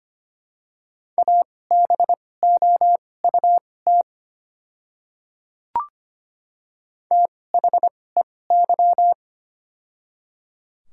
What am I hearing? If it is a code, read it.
Morse code, heard as a keyed tone, about 25 wpm, spaced as for 18 wpm, 700 Hz. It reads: ABOUT THEY
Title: 2x Character Spacing - Top 300 Words - 25wpm